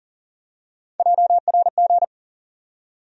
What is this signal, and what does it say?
Morse code, heard as a keyed tone, about 40 wpm, 700 Hz. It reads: JRG
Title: USA Call Sign Postfixes - 3 Letters 40wpm